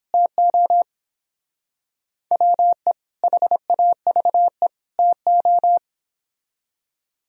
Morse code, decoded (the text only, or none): TO WE HAVE TO